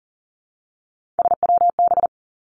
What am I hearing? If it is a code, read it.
Morse code, heard as a keyed tone, about 38 wpm, 700 Hz. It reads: SWB
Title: Three Challenging Letter-Number Combinations 40wpm